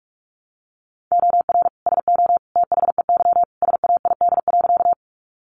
Morse code, heard as a keyed tone, about 44 wpm, 700 Hz. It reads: OR SO THEY SAID.
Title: Sentences from Top 100 Words 45wpm